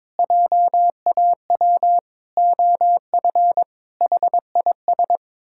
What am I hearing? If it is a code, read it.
Morse code, heard as a keyed tone, about 22 wpm, 700 Hz. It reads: JAW OF HIS
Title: Pangram Challenge v2 - 22wpm